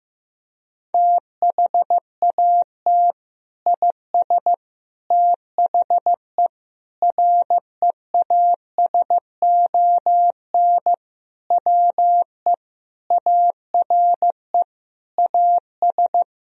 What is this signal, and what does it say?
Morse code, heard as a keyed tone, about 16 wpm, 700 Hz. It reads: THAT IS THE REASON WE ARE AS
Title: Sentences from Top 1000 Words - Review Part 4 - 15wpm